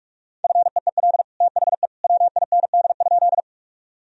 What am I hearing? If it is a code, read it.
Morse code, heard as a keyed tone, about 42 wpm, 700 Hz. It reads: FEEL THE WIND?
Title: Sentences from Top 1000 Words - Review Part 2 - 45wpm